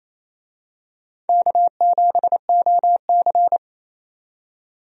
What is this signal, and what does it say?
Morse code, heard as a keyed tone, about 27 wpm, 700 Hz. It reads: K7OC
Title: USA Call Signs - 4 Characters - 28wpm